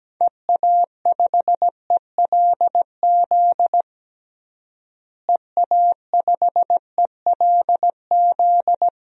Morse code, heard as a keyed tone, about 17 wpm, 700 Hz. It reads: EA5ELZ EA5ELZ
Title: DXCC-100 Call Signs - 6 Characters - Repeated 3x - 17wpm